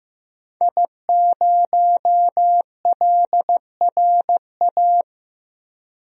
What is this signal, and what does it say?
Morse code, heard as a keyed tone, about 15 wpm, 700 Hz. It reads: I0LRA
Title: DXCC-100 Call Signs - 5 Characters - Repeated 3x - 15wpm